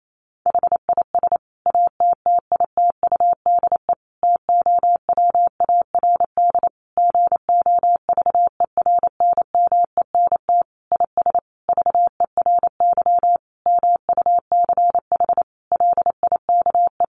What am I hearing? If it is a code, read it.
Morse code, heard as a keyed tone, about 28 wpm, 700 Hz. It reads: HIS ATTITUDE TOWARD GOVERNMENT IS VERY MUCH LIKE
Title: Sentences from Top 1300 Words 28wpm